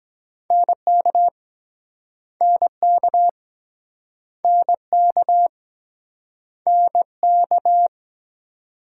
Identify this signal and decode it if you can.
Morse code, heard as a keyed tone, about 23 wpm, 700 Hz. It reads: NK NK NK NK